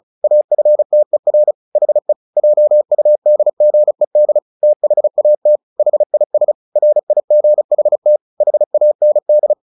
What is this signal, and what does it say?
Morse code, heard as a keyed tone, about 34 wpm, 600 Hz. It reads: AFTER HE JUDGED THAT HIS RIGHT HAND
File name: the old man and the sea 2 35wpm